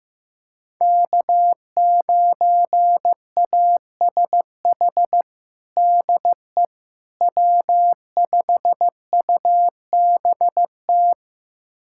Morse code, heard as a keyed tone, about 15 wpm, 700 Hz. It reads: K9ASH DE W5UBT
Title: USA Callsign DE Callsign - 5 Characters - 15wpm